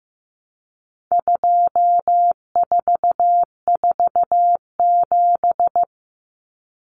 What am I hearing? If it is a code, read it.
Morse code, heard as a keyed tone, about 15 wpm, 700 Hz. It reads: 2447
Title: Numbers - 4-Digit 15wpm